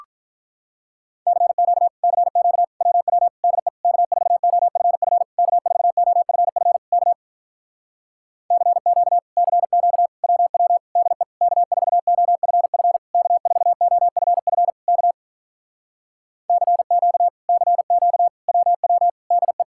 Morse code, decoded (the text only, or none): CQ CQ WW DE K4OFF K4OFF K CQ CQ WW DE K4OFF K4OFF K CQ CQ WW DE